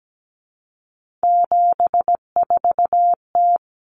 Morse code, heard as a keyed tone, about 17 wpm, 690 Hz. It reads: 74T
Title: Common Cut Numbers - 3-Digit 17wpm